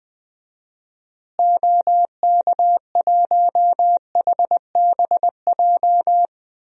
Morse code, heard as a keyed tone, about 19 wpm, 700 Hz. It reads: OK1HBJ
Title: DXCC-100 Call Signs - 6 Characters - Repeated 3x - 20wpm